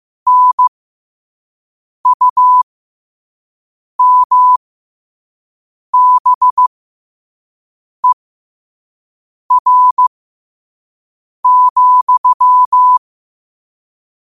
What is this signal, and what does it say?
Morse code, heard as a keyed tone, about 15 wpm, spaced as for 5 wpm, 1000 Hz. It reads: NUMBER,
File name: 2026-03-16-5wpm